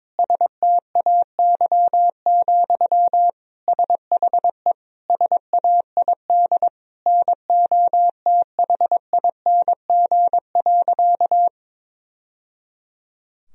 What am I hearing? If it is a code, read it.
Morse code, heard as a keyed tone, about 22 wpm, 700 Hz. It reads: STAY, SHE SAID NOTHING.
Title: Sentences from Top 1000 Words - Review Part 2 - 22wpm